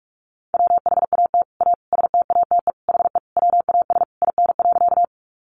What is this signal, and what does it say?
Morse code, heard as a keyed tone, about 44 wpm, 700 Hz. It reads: WHAT A STATE HE WAS IN.
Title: Sentences from Top 400 Words 45wpm